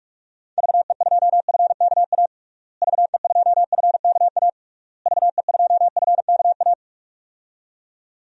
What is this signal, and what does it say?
Morse code, heard as a keyed tone, about 46 wpm, 700 Hz. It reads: VE2FKA VE2FKA VE2FKA
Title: DXCC-100 Call Signs - 6 Characters - Repeated 3x - 45wpm